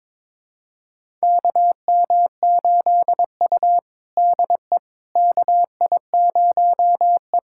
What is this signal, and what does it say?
Morse code, heard as a keyed tone, about 23 wpm, 700 Hz. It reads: KM8U DE KI0E
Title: USA Callsign DE Callsign - 4 Characters - 22wpm